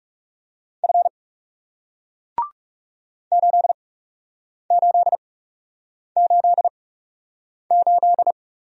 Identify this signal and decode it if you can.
Morse code, heard as a keyed tone, about 39 wpm, spaced as for 8 wpm, 700 Hz. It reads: F 8888